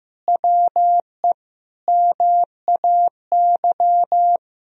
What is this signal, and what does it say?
Morse code, heard as a keyed tone, about 15 wpm, 700 Hz. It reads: WE MAY
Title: Sentences from Top 1000 Words - Review Part 4 - 15wpm